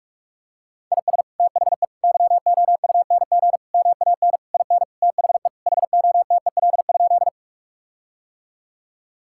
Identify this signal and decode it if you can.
Morse code, heard as a keyed tone, about 46 wpm, 700 Hz. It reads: IS THE YOUNG MAN IN THE HOTEL?
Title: Sentences from Top 1600 Words 45wpm